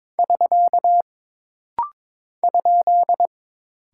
Morse code, heard as a keyed tone, about 22 wpm, 700 Hz. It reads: <SK> ?
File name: practice / 22wpm / Single Punctuation-Prosign - Rapid-Fire - 22wpm